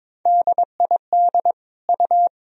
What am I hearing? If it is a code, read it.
Morse code, heard as a keyed tone, about 22 wpm, 700 Hz. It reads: DID U
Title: Sentences from Top 100 Words and Abbreviations 22wpm